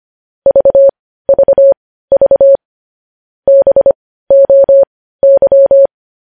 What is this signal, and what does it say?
Morse code, heard as a keyed tone, about 25 wpm, spaced as for 15 wpm, 550 Hz. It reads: VVV BOY
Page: news-headlines-in-morse-code-at-15-wpm